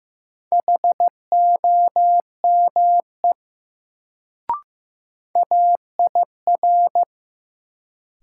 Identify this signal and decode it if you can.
Morse code, heard as a keyed tone, about 15 wpm, 700 Hz. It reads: HOME AIR
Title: Sentences from Top 200 Words - New Word Drill - 15wpm